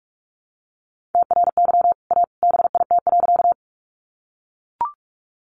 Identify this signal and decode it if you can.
Morse code, heard as a keyed tone, about 44 wpm, 700 Hz. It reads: TRY A BIT.
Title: Sentences from Top 900 Words 45wpm